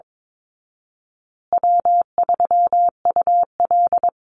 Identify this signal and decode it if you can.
Morse code, heard as a keyed tone, about 22 wpm, 700 Hz. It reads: W3UL
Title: Easy Call Signs Repeated 3x - 22wpm